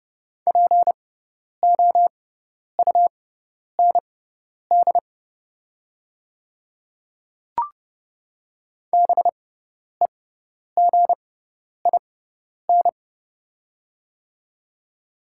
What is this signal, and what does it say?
Morse code, heard as a keyed tone, about 30 wpm, spaced as for 10 wpm, 690 Hz. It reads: POUND BEGIN
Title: 6x Character Spacing - Top 500 Words - 30wpm